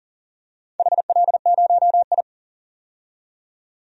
Morse code, heard as a keyed tone, about 40 wpm, 690 Hz. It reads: HL0I
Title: DXCC-100 Call Signs - 4 Characters - 40wpm